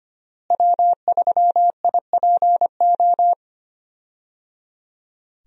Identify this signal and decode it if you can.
Morse code, heard as a keyed tone, about 25 wpm, 700 Hz. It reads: W3IPO
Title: USA Call Signs - 5 Characters - Repeated 3x - 25wpm